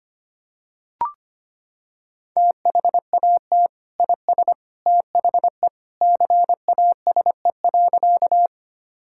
Morse code, heard as a keyed tone, about 25 wpm, 700 Hz. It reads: THAT IS THE CASE.
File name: practice / 25wpm / Sentences with 2-Word Phrases - 25wpm